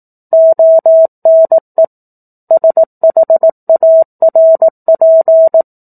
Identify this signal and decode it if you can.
Morse code, heard as a keyed tone, about 18 wpm, 650 Hz. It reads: ONE SHARP
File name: 04-Chapter-04 18 WPM